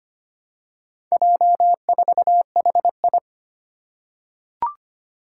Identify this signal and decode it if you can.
Morse code, heard as a keyed tone, about 25 wpm, 700 Hz. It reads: J4HI
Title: Four Challenging Letter-Number Combinations 25wpm